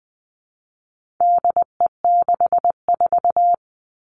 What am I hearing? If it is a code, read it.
Morse code, heard as a keyed tone, about 20 wpm, 700 Hz. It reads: DE64
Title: All Cut Numbers - 4-Digit 20wpm